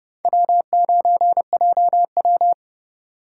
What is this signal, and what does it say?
Morse code, heard as a keyed tone, about 29 wpm, 700 Hz. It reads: W9JW